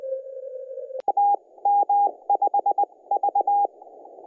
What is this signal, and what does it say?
Morse code, heard as a keyed tone, about 20 wpm, spaced as for 16 wpm, 810 Hz, with radio noise. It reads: TM5V
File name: N0TG-CY0-BRQ-160